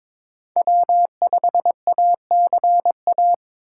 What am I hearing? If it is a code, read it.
Morse code, heard as a keyed tone, about 22 wpm, 700 Hz. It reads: W5ACA
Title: USA Call Signs - 3x Character Spacing - 22wpm